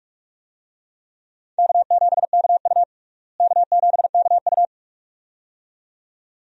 Morse code, heard as a keyed tone, about 44 wpm, 700 Hz. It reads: K7KU K7KU
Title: USA Call Signs - 4 Characters - Repeated 2x - 45wpm